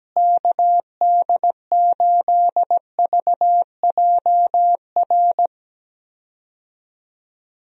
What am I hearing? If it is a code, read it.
Morse code, heard as a keyed tone, about 17 wpm, 700 Hz. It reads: KD8VJR
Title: USA Call Signs - 6 Characters - Repeated 2x - 17wpm